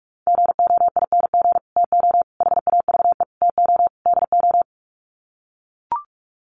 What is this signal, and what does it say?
Morse code, heard as a keyed tone, about 44 wpm, 700 Hz. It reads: GOING TO HAVE TO DO
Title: Sets of 5 Words Encore 45wpm